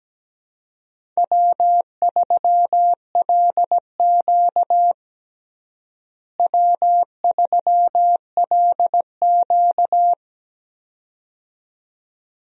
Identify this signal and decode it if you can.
Morse code, heard as a keyed tone, about 17 wpm, 700 Hz. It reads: W3LQ W3LQ
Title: USA Call Signs - 4 Characters - Repeated 2x - 17wpm